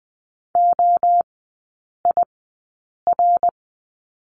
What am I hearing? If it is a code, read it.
Morse code, heard as a keyed tone, about 20 wpm, spaced as for 8 wpm, 700 Hz. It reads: OIR
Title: 121 Lesson 20e - Number 3 - Three Characters 20wpm